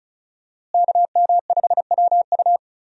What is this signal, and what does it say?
Morse code, heard as a keyed tone, about 36 wpm, 700 Hz. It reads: KM5WU
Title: USA Call Signs - 5 Characters - Repeated 2x - 35wpm